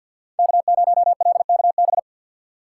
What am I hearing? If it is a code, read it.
Morse code, heard as a keyed tone, about 53 wpm, 700 Hz. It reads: K0LKB
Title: USA Call Signs - 5 Characters - 3x Character Spacing - 50wpm